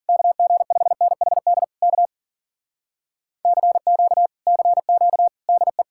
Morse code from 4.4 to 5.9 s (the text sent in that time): CQ DE